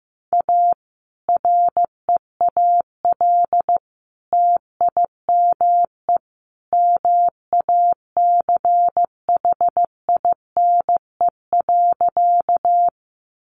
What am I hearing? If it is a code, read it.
Morse code, heard as a keyed tone, about 15 wpm, 700 Hz. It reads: A REAL TIME MACHINE.
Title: A Sound Of Thunder 15wpm